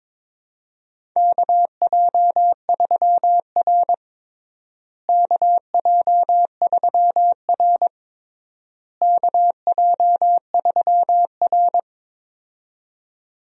Morse code, heard as a keyed tone, about 22 wpm, 700 Hz. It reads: KJ3R KJ3R KJ3R
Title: USA Call Signs - 4 Characters - Repeated 3x - 22wpm